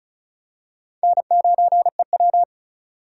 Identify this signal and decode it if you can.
Morse code, heard as a keyed tone, about 36 wpm, 700 Hz. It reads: N9EW
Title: USA Call Signs - 4 Characters - 3x Character Spacing - 35wpm